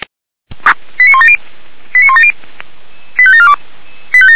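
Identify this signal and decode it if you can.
Five-tone selective call (ZVEI), heard as quick runs of five tones, tones of about 70 ms. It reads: 89170 89170 85742
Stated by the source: ZVEI five-tone selective call received on an SDR, fire brigade alerting.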